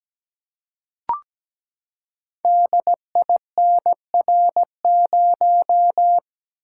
Morse code, heard as a keyed tone, about 17 wpm, 700 Hz. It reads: DINR0
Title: Code Groups - 5 Letter:Numbers - 17wpm